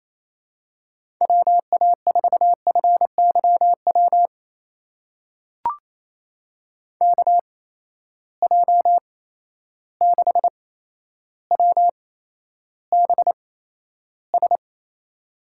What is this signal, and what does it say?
Morse code, heard as a keyed tone, about 28 wpm, spaced as for 12 wpm, 700 Hz. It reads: WA4FYW KJ6WBS
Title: USA Call Signs - 6 Characters - 8x Character Spacing - 28wpm